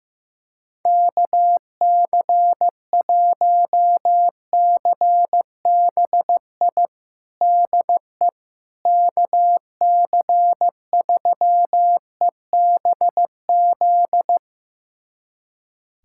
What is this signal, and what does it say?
Morse code, heard as a keyed tone, about 15 wpm, 700 Hz. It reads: KC1CBI DE KC3EBZ
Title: USA Callsign DE Callsign - 6 Characters - 15wpm